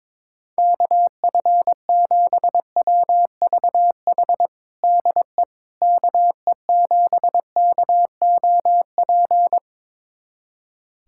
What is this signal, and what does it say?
Morse code, heard as a keyed tone, about 22 wpm, 700 Hz. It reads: KF7WVH DE KE7KOP
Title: USA Callsign DE Callsign - 6 Characters - 22wpm